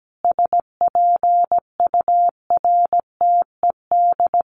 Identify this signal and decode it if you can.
Morse code, heard as a keyed tone, about 17 wpm, 700 Hz. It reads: SPURTED
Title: A Sound Of Thunder 17wpm